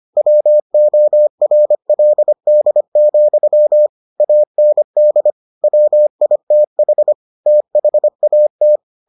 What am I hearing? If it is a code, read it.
Morse code, heard as a keyed tone, about 25 wpm, 600 Hz. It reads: WORLD, AND WITH THAT